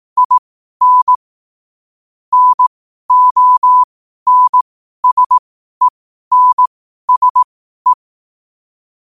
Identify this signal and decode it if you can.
Morse code, heard as a keyed tone, about 18 wpm, spaced as for 12 wpm, 1000 Hz. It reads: IN NONSENSE
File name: qotdfiles / 12wpm